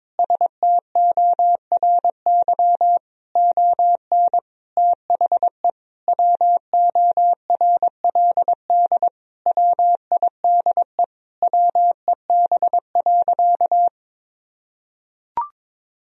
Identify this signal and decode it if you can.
Morse code, heard as a keyed tone, about 22 wpm, 700 Hz. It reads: STORY ON THE WORLD WIDE WEB.
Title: Sentences from Top 1000 Words - Review Part 4 - 22wpm